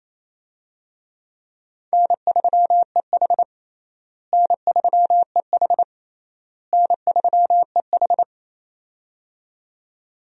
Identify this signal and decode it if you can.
Morse code, heard as a keyed tone, about 27 wpm, 700 Hz. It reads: N3EH N3EH N3EH
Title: USA Call Signs - 4 Characters - Repeated 3x - 28wpm